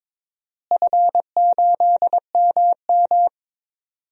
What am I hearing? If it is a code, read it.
Morse code, heard as a keyed tone, about 21 wpm, 700 Hz. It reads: F8MM
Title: DXCC-100 Call Signs - 4 Characters - Repeated 2x - 22wpm